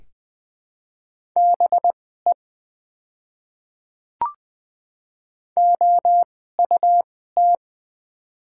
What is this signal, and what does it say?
Morse code, heard as a keyed tone, about 20 wpm, spaced as for 15 wpm, 700 Hz. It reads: BE OUT
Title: 2x Character Spacing - Top 100 Words - 20wpm